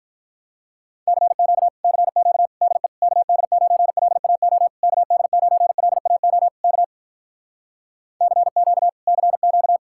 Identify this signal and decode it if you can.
Morse code, heard as a keyed tone, about 48 wpm, 700 Hz. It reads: CQ CQ DE KD9LAO KD9LAO K CQ CQ